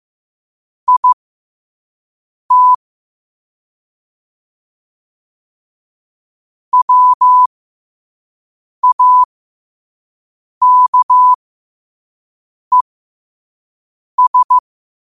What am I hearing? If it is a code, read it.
Morse code, heard as a keyed tone, about 15 wpm, spaced as for 5 wpm, 1000 Hz. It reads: IT WAKES
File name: 2024-10-07-5wpm